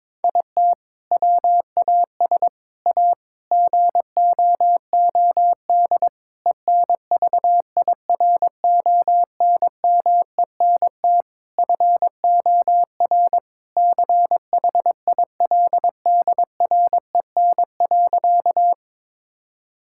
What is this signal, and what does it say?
Morse code, heard as a keyed tone, about 23 wpm, 700 Hz. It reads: IT WAS A GOOD ENVIRONMENT FOR CHILDREN.
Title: Sentences from Top 1000 Words - Review Part 5 - 22wpm